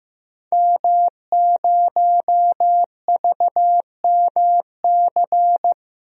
Morse code, heard as a keyed tone, about 15 wpm, 700 Hz. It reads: M0VMC
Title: DXCC-100 Call Signs - 5 Characters - Repeated 2x - 15wpm